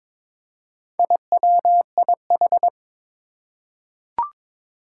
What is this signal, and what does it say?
Morse code, heard as a keyed tone, about 22 wpm, 700 Hz. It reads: IWIH
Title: Four Challenging Letter-Number Combinations 22wpm